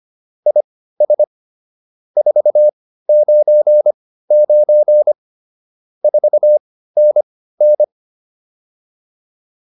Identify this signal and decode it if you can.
Morse code, heard as a keyed tone, about 25 wpm, spaced as for 15 wpm, 600 Hz. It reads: IS 499 4NN